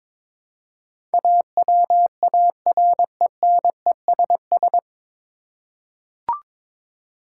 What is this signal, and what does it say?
Morse code, heard as a keyed tone, about 22 wpm, 700 Hz. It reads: AWARENESS
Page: Morse Code Ninja practice 22wpm